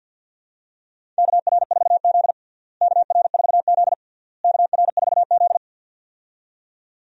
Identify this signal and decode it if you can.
Morse code, heard as a keyed tone, about 52 wpm, 700 Hz. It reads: KR4Z KR4Z KR4Z
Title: USA Call Signs - 4 Characters - Repeated 3x - 50wpm